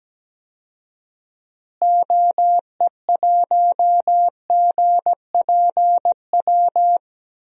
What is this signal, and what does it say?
Morse code, heard as a keyed tone, about 17 wpm, 700 Hz. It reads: OE1GPW
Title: DXCC-100 Call Signs - 6 Characters - Repeated 3x - 17wpm